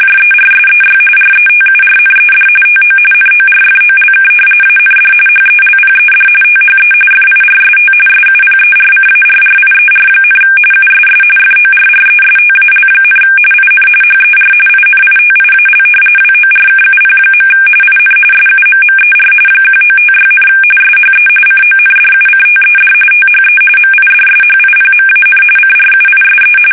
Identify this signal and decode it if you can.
RTTY, about 45.45 baud. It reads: JERSEY RETURNED TO NORFOLK AND HELPED THE FIRST SUPERCARRIER, USS FORRESTAL, IN HER INITIAL WORKUP. IN SUMMER 1956, SHE CONDUCTED THE MIDSHIPMAN CRUISE
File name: fsk850 / battleshipnj